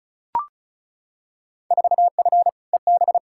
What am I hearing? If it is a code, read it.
Morse code, heard as a keyed tone, about 34 wpm, 700 Hz. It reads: 4F EB
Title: Field Day Exchanges - 35wpm